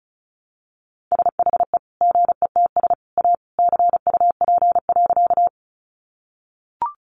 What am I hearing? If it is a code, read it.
Morse code, heard as a keyed tone, about 34 wpm, 700 Hz. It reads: SHE GETS A CUP.
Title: Easy Phrases 35wpm